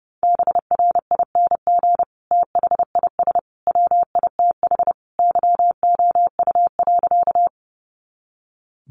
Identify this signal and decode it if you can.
Morse code, heard as a keyed tone, about 28 wpm, 700 Hz. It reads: BRING THIS WITH YOU.
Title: Sentences from Top 500 Words 30wpm